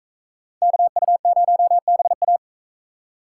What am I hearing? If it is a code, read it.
Morse code, heard as a keyed tone, about 44 wpm, 700 Hz. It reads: KU0BA